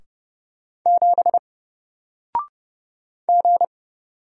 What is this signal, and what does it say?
Morse code, heard as a keyed tone, about 30 wpm, 700 Hz. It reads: 7 G
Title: Single Letter-Number 30wpm